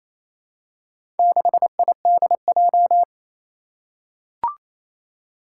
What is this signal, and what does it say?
Morse code, heard as a keyed tone, about 27 wpm, 700 Hz. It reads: 6IDJ
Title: Four Challenging Letter-Number Combinations 28wpm